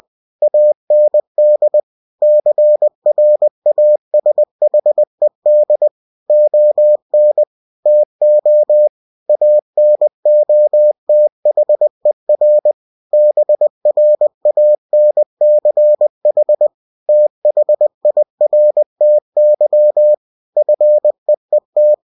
Morse code, heard as a keyed tone, about 20 wpm, 600 Hz. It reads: AND CRASHED ON TO ANOTHER BRANCH THIRTY FEET
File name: cw-books / winnie-the-pooh / winnie-20wpm-page-0003